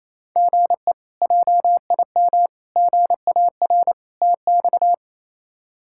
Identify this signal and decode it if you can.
Morse code, heard as a keyed tone, about 28 wpm, 700 Hz. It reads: GE JIM GAR TX
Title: K1USN SST QSO Practice - 28wpm